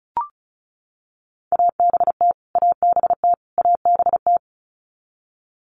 Morse code, heard as a keyed tone, about 34 wpm, 700 Hz. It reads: ABT ABT ABT
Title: Abbreviations Repeated 3x - 35wpm